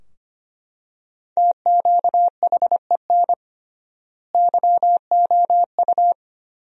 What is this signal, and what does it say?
Morse code, heard as a keyed tone, about 25 wpm, 700 Hz. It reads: TQHEN YOU
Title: 3x Word-Spacing - Sentences from Top 100 Words - 25wpm